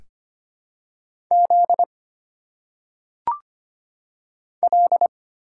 Morse code, heard as a keyed tone, about 25 wpm, 700 Hz. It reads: Z L
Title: Single Letters - Phonetic Pronunciation 25wpm